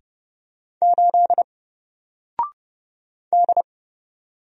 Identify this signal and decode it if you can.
Morse code, heard as a keyed tone, about 30 wpm, 700 Hz. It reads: 8 D
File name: Single-Letter-Number-30wpm